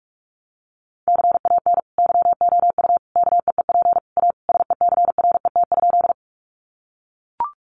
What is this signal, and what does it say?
Morse code, heard as a keyed tone, about 44 wpm, 700 Hz. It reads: CAN YOU KEEP A SECRET?
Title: Sentences from Top 1200 Words 45wpm